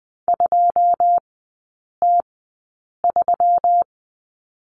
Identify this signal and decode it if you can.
Morse code, heard as a keyed tone, about 20 wpm, spaced as for 8 wpm, 700 Hz. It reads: 2T3
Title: 166 Lesson 27e - Number 7 - Three Characters 20wpm